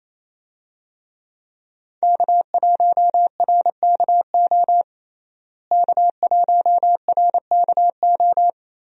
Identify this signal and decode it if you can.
Morse code, heard as a keyed tone, about 28 wpm, 700 Hz. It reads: K1RKO K1RKO
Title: USA Call Signs - 5 Characters - Repeated 2x - 28wpm